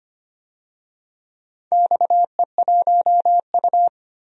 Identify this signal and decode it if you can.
Morse code, heard as a keyed tone, about 25 wpm, 700 Hz. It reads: XE1U
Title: DXCC-100 Call Signs - 4 Characters - 25wpm